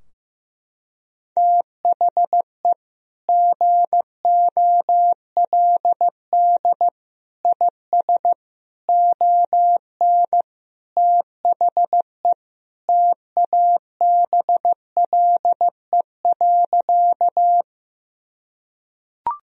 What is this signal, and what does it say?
Morse code, heard as a keyed tone, about 15 wpm, 700 Hz. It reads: THE GOLD IS ON THE TABLE.
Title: Sentences from Top 1000 Words - Review Part 3 - 15wpm